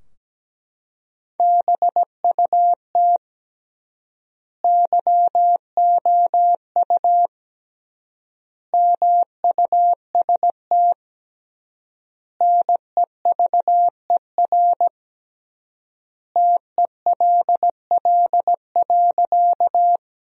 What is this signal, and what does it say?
Morse code, heard as a keyed tone, about 17 wpm, 700 Hz. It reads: BUT YOU MUST NEVER TELL.
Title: 3x Word-Spacing - Sentences from Top 300 Words - 17wpm